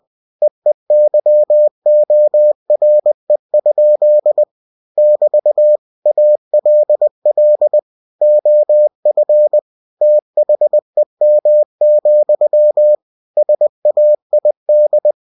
Morse code, heard as a keyed tone, about 20 wpm, 600 Hz. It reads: EEYORE? <BT> ALL OF THEM, SAID
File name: cw-books / winnie-the-pooh / winnie-20wpm-page-0105